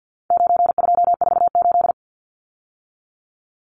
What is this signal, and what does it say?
Morse code, heard as a keyed tone, about 50 wpm, 700 Hz. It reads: 9248
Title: Numbers - 4-Digit 50wpm